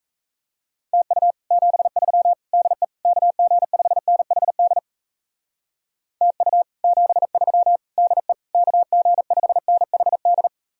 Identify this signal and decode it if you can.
Morse code, heard as a keyed tone, about 37 wpm, 700 Hz. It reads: TU 73 DE KG5NHD TU 73 DE KG5NHD